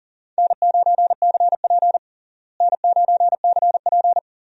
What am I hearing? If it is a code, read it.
Morse code, heard as a keyed tone, about 37 wpm, 700 Hz. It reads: N9CP N9CP
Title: USA Call Signs - 4 Characters - Repeated 2x - 40wpm